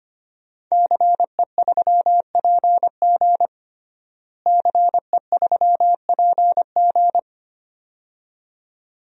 Morse code, heard as a keyed tone, about 25 wpm, 700 Hz. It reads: CE3PG CE3PG
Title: DXCC-100 Call Signs - 5 Characters - Repeated 2x - 25wpm